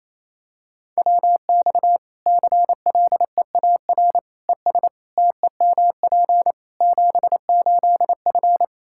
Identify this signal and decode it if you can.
Morse code, heard as a keyed tone, about 27 wpm, 700 Hz. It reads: WX CLEAR ES TEMP 78F
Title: K1USN SST QSO Practice 28wpm